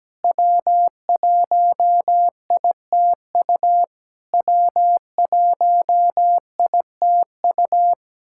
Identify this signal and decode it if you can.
Morse code, heard as a keyed tone, about 17 wpm, 700 Hz. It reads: W1ITU W1ITU
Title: SOTA QSO Practice 17wpm